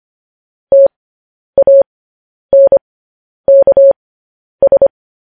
Morse code, heard as a keyed tone, about 25 wpm, spaced as for 10 wpm, 550 Hz. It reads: TANKS
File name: podcast / News headlines-1x-25x10